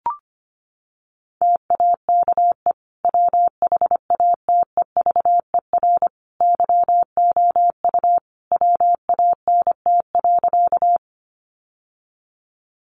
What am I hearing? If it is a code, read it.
Morse code, heard as a keyed tone, about 25 wpm, 700 Hz. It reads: TAKE WHATEVER YOU WANT.
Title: Sentences from Top 1100 Words 25wpm